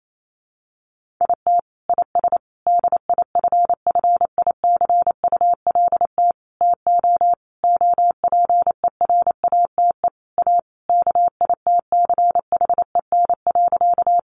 Morse code, heard as a keyed tone, about 28 wpm, 700 Hz. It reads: IT IS DIFFICULT TO OPERATE A KITCHEN.
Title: Sentences from Top 1500 Words 28wpm